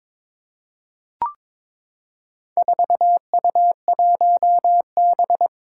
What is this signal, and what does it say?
Morse code, heard as a keyed tone, about 22 wpm, 700 Hz. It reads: 4U1B
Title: Four Challenging Letter-Number Combinations 22wpm